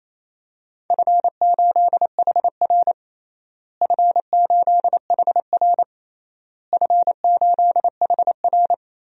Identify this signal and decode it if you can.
Morse code, heard as a keyed tone, about 28 wpm, 700 Hz. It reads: F8HR F8HR F8HR
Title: DXCC-100 Call Signs - 4 Characters - Repeated 3x - 28wpm